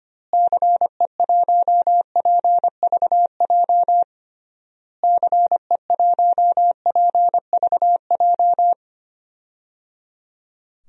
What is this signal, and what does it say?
Morse code, heard as a keyed tone, about 25 wpm, 700 Hz. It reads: CE1PVJ CE1PVJ
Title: DXCC-100 Call Signs - 6 Characters - Repeated 3x - 25wpm